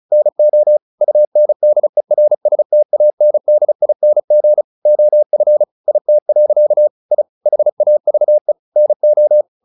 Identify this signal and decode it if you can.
Morse code, heard as a keyed tone, about 35 wpm, 600 Hz. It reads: NO UNDERSTANDING OF IT. I HAVE NO
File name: the old man and the sea 2 35wpm